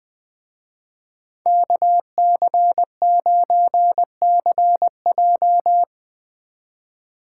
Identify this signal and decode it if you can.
Morse code, heard as a keyed tone, about 19 wpm, 700 Hz. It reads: KC9CJ
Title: Speed Racing - USA 2x2 Call Signs 17wpm